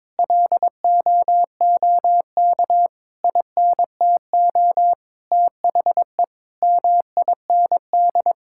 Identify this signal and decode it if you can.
Morse code, heard as a keyed tone, about 22 wpm, 700 Hz. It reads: LOOK INTO THE MIND
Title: Sentences from Top 1600 Words 22wpm